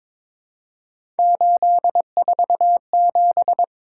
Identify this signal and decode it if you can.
Morse code, heard as a keyed tone, about 22 wpm, 700 Hz. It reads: 847
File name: practice / 22wpm / Three Letter-Number Combinations - 22wpm